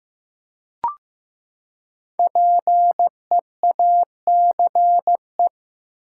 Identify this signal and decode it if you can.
Morse code, heard as a keyed tone, about 15 wpm, 700 Hz. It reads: PEACE
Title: Five Letter Words - 15wpm